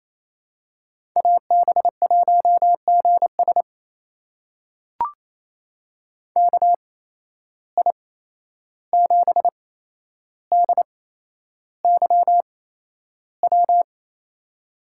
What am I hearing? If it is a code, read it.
Morse code, heard as a keyed tone, about 28 wpm, spaced as for 11 wpm, 700 Hz. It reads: AB1GS KI7DYW
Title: USA Call Signs - 8x Character Spacing - 28wpm